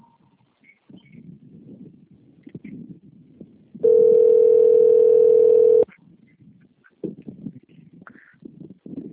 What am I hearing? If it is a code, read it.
Ringback tone (North America), 440 + 480 Hz.